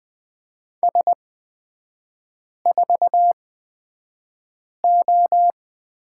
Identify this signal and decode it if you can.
Morse code, heard as a keyed tone, about 20 wpm, spaced as for 5 wpm, 700 Hz. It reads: S4O
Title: 224 Lesson 36e - BT - Three Characters 20wpm